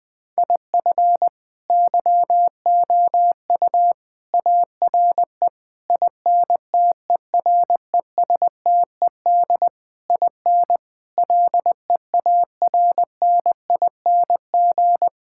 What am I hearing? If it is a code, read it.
Morse code, heard as a keyed tone, about 21 wpm, 700 Hz. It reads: IF YOU ARE INTERESTED IN LEARNING
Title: Sentences with 8-Word Phrases - 20wpm